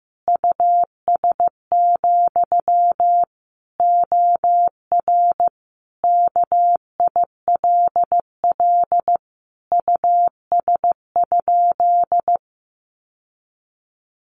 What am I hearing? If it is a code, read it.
Morse code, heard as a keyed tone, about 15 wpm, 700 Hz. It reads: US, OR KILL US?
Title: Sentences from Top 1400 Words 15wpm